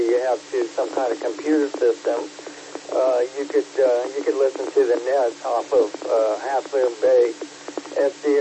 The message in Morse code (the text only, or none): none